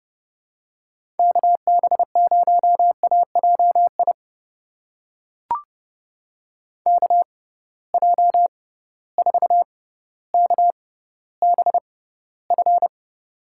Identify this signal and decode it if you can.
Morse code, heard as a keyed tone, about 30 wpm, spaced as for 15 wpm, 700 Hz. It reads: KB0AJI KJ4KBF
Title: USA Call Signs - 6 Characters - 6x Character Spacing - 30wpm